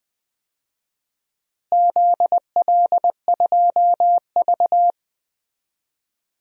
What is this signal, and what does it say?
Morse code, heard as a keyed tone, about 20 wpm, 700 Hz. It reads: ZL2V
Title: DXCC-100 Call Signs - 4 Characters - Repeated 2x - 20wpm